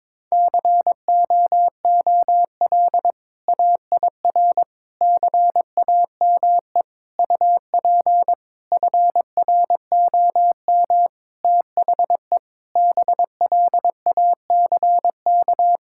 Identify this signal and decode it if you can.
Morse code, heard as a keyed tone, about 22 wpm, 700 Hz. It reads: COOL AIR CAME UP FROM THE BLACK
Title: Sentences from Top 1600 Words 22wpm